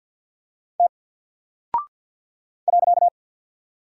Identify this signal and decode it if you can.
Morse code, heard as a keyed tone, about 48 wpm, 700 Hz. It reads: T .
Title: Single Character (Letter:Number:Punctuation:Prosign) Alt 50wpm